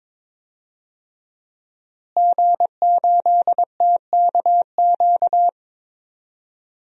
Morse code, heard as a keyed tone, about 21 wpm, 700 Hz. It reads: G8TKQ
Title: DXCC-100 Call Signs - 5 Characters - Repeated 2x - 22wpm